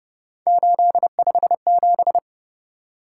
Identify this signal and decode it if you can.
Morse code, heard as a keyed tone, about 29 wpm, 700 Hz. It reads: 857